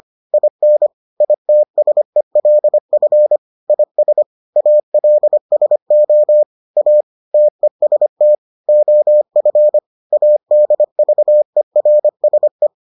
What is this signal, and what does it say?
Morse code, heard as a keyed tone, about 25 wpm, 600 Hz. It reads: IN ITSELF IS ALSO A TEST OF ADVERSE